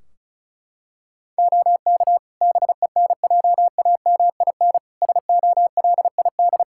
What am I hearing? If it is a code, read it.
Morse code, heard as a keyed tone, about 36 wpm, 700 Hz. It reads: OK BENJAMIN SOLID
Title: Practice QSO - Exchange 3 - More Info - 35wpm